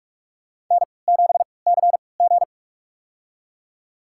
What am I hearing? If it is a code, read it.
Morse code, heard as a keyed tone, about 45 wpm, spaced as for 25 wpm, 700 Hz. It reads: N7CG
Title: USA Call Signs - 4 Characters - 3x Character Spacing - 45wpm